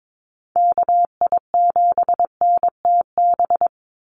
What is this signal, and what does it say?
Morse code, heard as a keyed tone, about 22 wpm, 700 Hz. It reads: KI7NTB
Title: USA Call Signs 22wpm